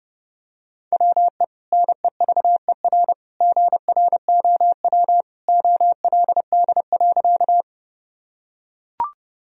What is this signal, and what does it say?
Morse code, heard as a keyed tone, about 32 wpm, 700 Hz. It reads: WE NEVER GROW OLD.
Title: Sentences from Top 1000 Words - Review Part 1 - 30wpm